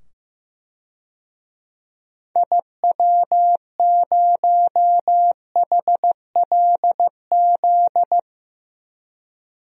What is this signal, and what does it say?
Morse code, heard as a keyed tone, about 15 wpm, 690 Hz. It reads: IW0HLZ
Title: DXCC-100 Call Signs - 6 Characters - 15wpm